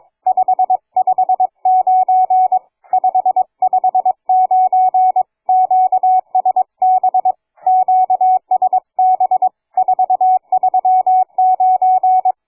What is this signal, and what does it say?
Morse code, heard as a keyed tone, about 22 wpm, 740 Hz, with radio noise. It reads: 559 559 QSB QSB 439
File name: cw forum / fg5fr 160m 02h30